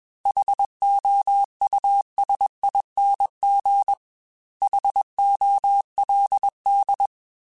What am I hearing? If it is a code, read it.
Morse code, heard as a keyed tone, about 21 wpm, 800 Hz. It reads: HOUSING HOLD